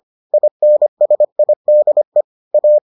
Morse code, heard as a keyed tone, about 25 wpm, 600 Hz. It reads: INSIDE A